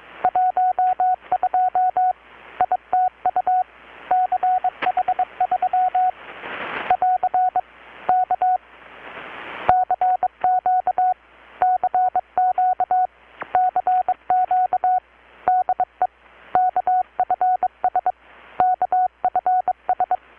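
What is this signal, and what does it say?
Morse code, heard as a keyed tone, about 22 wpm, 720 Hz. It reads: 12 ITU CH3 <AR> K CQ CQ CQ DE KFS KFS
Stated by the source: Morse from KPH's Night of Nights XX maritime radiotelegraphy event, received in Chiapas, Mexico about 3,800 km away on a Kenwood R-600 with a grounded 10 m long-wire antenna.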